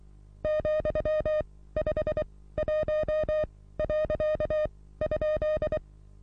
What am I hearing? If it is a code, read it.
Morse code, heard as a keyed tone, about 23 wpm, spaced as for 16 wpm, 620 Hz. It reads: ,51.?
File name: repos / Morsekurs / DJ4UF Morsen27